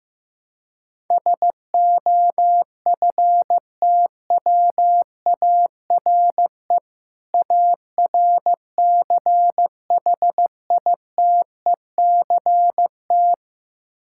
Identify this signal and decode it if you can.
Morse code, heard as a keyed tone, about 15 wpm, 700 Hz. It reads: SOFTWARE ARCHITECT
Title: Practice QSO - Exchange 3 - More Info - 15wpm